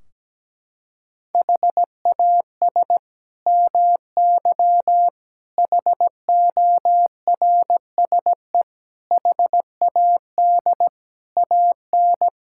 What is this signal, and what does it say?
Morse code, heard as a keyed tone, about 17 wpm, 700 Hz. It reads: HAS MY HORSE HAD AN
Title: Sentences from Top 1600 Words 17wpm